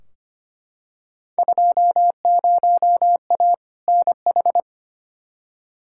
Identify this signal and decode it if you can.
Morse code, heard as a keyed tone, about 25 wpm, 700 Hz. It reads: 20A NH
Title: Field Day Exchanges Encore - 25wpm